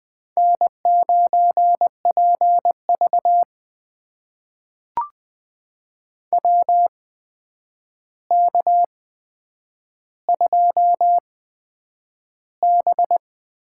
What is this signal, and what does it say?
Morse code, heard as a keyed tone, about 20 wpm, spaced as for 9 wpm, 700 Hz. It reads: N9PV WK2B
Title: USA Call Signs - 4 Characters - 8x Character Spacing - 20wpm